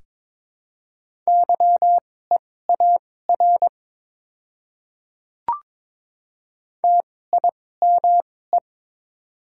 Morse code, heard as a keyed tone, about 22 wpm, spaced as for 16 wpm, 700 Hz. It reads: YEAR TIME
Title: 2x Character Spacing - Top 300 Words - 22wpm